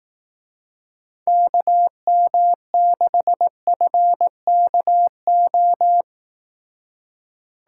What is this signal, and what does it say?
Morse code, heard as a keyed tone, about 18 wpm, 700 Hz. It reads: KM6FKO
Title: Speed Racing - USA Call Signs 15wpm